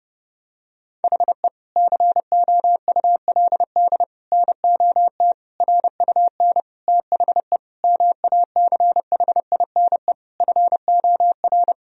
Morse code, heard as a keyed tone, about 30 wpm, 700 Hz. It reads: HE COULD NOT RUN THE MACHINE FOR
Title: Sentences from Top 1000 Words - Review Part 2 - 30wpm